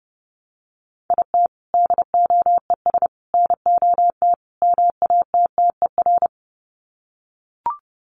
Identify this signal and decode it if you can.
Morse code, heard as a keyed tone, about 30 wpm, 700 Hz. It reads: IT DOES NOT MATTER
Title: Sets of 4 Words 30wpm